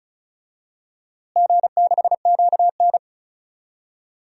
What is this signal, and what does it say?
Morse code, heard as a keyed tone, about 36 wpm, 700 Hz. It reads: G6QN
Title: DXCC-100 Call Signs - 4 Characters - Repeated 3x - 35wpm